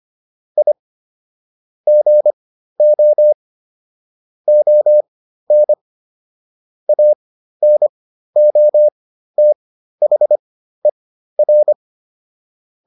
Morse code, heard as a keyed tone, about 25 wpm, spaced as for 13 wpm, 600 Hz. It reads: I GO ON ANOTHER